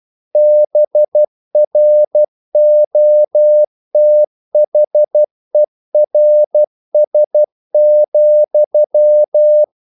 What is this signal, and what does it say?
Morse code, heard as a keyed tone, about 12 wpm, 600 Hz. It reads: BROTHERS,
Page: English quotations in CW (telegraphy - morse)